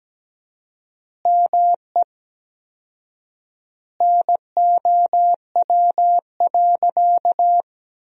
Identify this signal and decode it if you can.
Morse code, heard as a keyed tone, about 17 wpm, 700 Hz. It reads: ME NOW.
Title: 4x Word-Spacing - Sentences from Top 200 Words - 17wpm